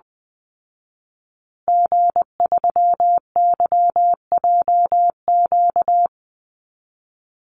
Morse code, heard as a keyed tone, about 20 wpm, 700 Hz. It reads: G3YJQ
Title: Call Signs Repeated 2x - 20wpm